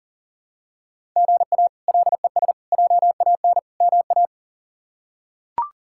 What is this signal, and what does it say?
Morse code, heard as a keyed tone, about 40 wpm, 700 Hz. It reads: GA LES JAN MA
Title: K1USN SST QSO Practice - 40wpm